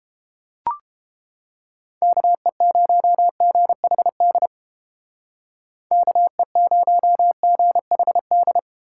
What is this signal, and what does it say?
Morse code, heard as a keyed tone, about 30 wpm, 700 Hz. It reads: KE0GHD KE0GHD